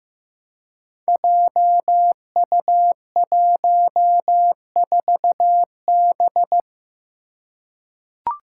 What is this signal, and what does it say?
Morse code, heard as a keyed tone, about 15 wpm, 700 Hz. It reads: JU14B
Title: Five Challenging Letter-Number Combinations 15wpm